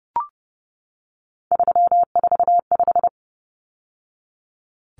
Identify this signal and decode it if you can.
Morse code, heard as a keyed tone, about 30 wpm, 700 Hz. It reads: 345
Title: Numbers - 3-Digit 30wpm